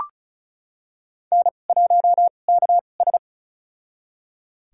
Morse code, heard as a keyed tone, about 36 wpm, spaced as for 25 wpm, 700 Hz. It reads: N1KS
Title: USA Call Signs - 4 Characters - 2x Character Spacing - 35wpm